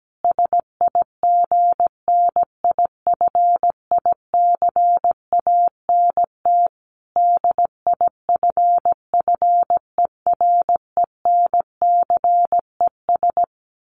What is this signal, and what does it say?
Morse code, heard as a keyed tone, about 17 wpm, 700 Hz. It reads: SIGNIFICANT DIFFERENCES
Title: Sets of 5 Words Encore2 17wpm